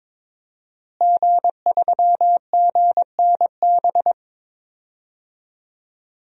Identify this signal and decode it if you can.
Morse code, heard as a keyed tone, about 22 wpm, 700 Hz. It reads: G3GNB
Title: DXCC-100 Call Signs - 5 Characters - Repeated 3x - 22wpm